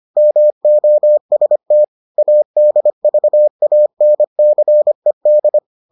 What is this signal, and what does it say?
Morse code, heard as a keyed tone, about 25 wpm, 600 Hz. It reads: MOST ADVANCED